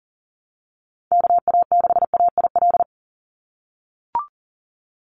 Morse code, heard as a keyed tone, about 38 wpm, 700 Hz. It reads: KA6AIL
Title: Call Signs - 40wpm